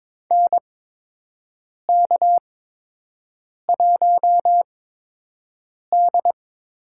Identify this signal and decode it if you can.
Morse code, heard as a keyed tone, about 22 wpm, spaced as for 6 wpm, 700 Hz. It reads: NK1D
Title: USA Call Signs - 8x Character Spacing - 22wpm